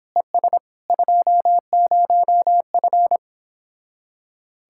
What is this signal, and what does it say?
Morse code, heard as a keyed tone, about 26 wpm, 700 Hz. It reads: ES 20F